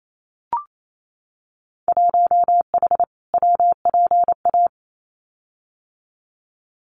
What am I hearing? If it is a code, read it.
Morse code, heard as a keyed tone, about 28 wpm, 700 Hz. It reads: 1H WPA
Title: Winter Field Day Exchanges 28wpm